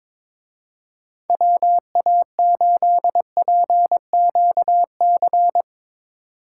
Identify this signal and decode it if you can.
Morse code, heard as a keyed tone, about 24 wpm, 700 Hz. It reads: WA8PQC
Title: Speed Racing - USA 2x3 Call Signs 22wpm